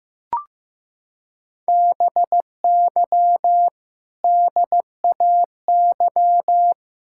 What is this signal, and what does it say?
Morse code, heard as a keyed tone, about 15 wpm, 700 Hz. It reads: BY DAY
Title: Sentences from Top 1600 Words 15wpm